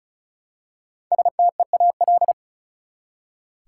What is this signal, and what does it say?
Morse code, heard as a keyed tone, about 36 wpm, 700 Hz. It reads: STEAL
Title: Five Letter Words - 35wpm